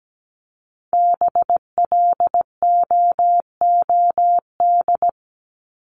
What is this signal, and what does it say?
Morse code, heard as a keyed tone, about 17 wpm, 700 Hz. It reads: BLOOD
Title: Top 1001-1250 Words 17wpm